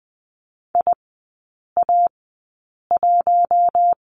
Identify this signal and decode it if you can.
Morse code, heard as a keyed tone, about 20 wpm, spaced as for 8 wpm, 700 Hz. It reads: IA1
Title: 081 Lesson 14e - Number 2 - Three Characters 20wpm